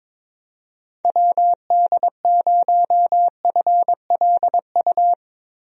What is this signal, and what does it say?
Morse code, heard as a keyed tone, about 22 wpm, 700 Hz. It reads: WD0FLU
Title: USA Call Signs - 6 Characters - Repeated 3x - 22wpm